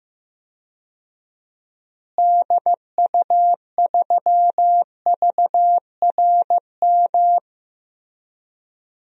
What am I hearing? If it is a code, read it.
Morse code, heard as a keyed tone, about 15 wpm, 700 Hz. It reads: DU3VRM
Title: DXCC-100 Call Signs - 6 Characters - Repeated 3x - 15wpm